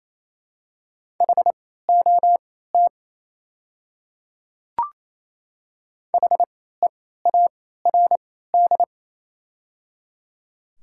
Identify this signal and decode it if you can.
Morse code, heard as a keyed tone, about 28 wpm, spaced as for 16 wpm, 690 Hz. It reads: HOT HEARD
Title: 3x Character Spacing - Top 500 Words - 28wpm